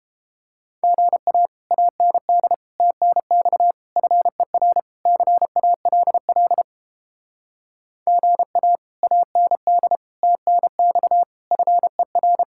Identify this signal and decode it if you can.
Morse code, heard as a keyed tone, about 32 wpm, 700 Hz. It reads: GA AND TNX FER CALL GA AND TNX FER